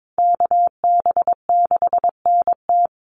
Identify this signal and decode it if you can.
Morse code, heard as a keyed tone, about 22 wpm, 700 Hz. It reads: KB6NT
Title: USA Call Signs Repeated 3x 22wpm